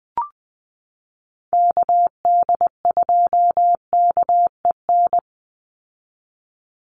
Morse code, heard as a keyed tone, about 20 wpm, 700 Hz. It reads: KD2KEN
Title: USA Call Signs 20wpm